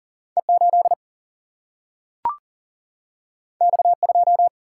Morse code, heard as a keyed tone, about 40 wpm, 700 Hz. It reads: E8 X2
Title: Single-Letter Single-Number 40wpm